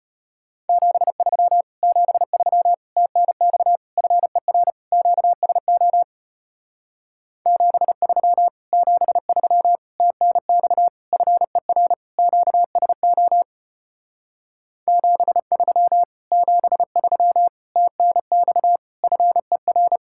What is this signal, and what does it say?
Morse code, heard as a keyed tone, about 33 wpm, 700 Hz. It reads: 73 73 TNX FER QSO 73 73 TNX FER QSO 73 73 TNX FER